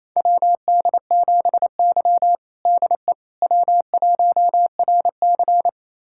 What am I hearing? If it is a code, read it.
Morse code, heard as a keyed tone, about 28 wpm, 700 Hz. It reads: WD7Y DE W1RC
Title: USA Callsign DE Callsign - 4 Characters - 28wpm